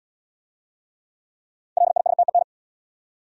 Morse code, heard as a keyed tone, about 68 wpm, 700 Hz. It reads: LISTEN